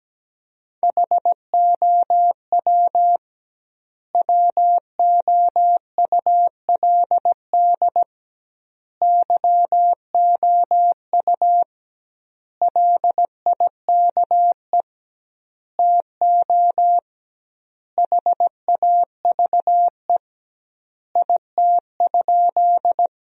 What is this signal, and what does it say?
Morse code, heard as a keyed tone, about 17 wpm, 700 Hz. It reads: HOW WOULD YOU LIKE TO HAVE IT?
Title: 2x Word-Spacing - Sentences from Top 100 Words - 17wpm